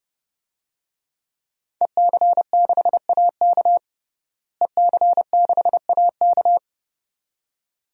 Morse code, heard as a keyed tone, about 29 wpm, 700 Hz. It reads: EC6AK EC6AK
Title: DXCC-100 Call Signs - 5 Characters - Repeated 2x - 30wpm